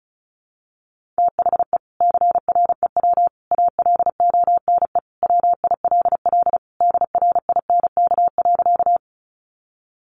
Morse code, heard as a keyed tone, about 34 wpm, 700 Hz. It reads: THE CREW ALONE WILL DRINK.
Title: Sentences from Top 1500 Words 35wpm